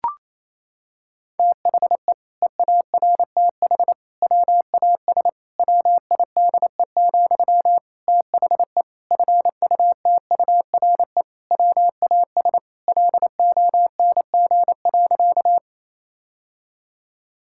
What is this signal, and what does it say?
Morse code, heard as a keyed tone, about 28 wpm, 700 Hz. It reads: THE EARTH WAS WIDE, THE FUTURE WAS LONG.
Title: Sentences from Top 1000 Words - Review Part 6 - 28wpm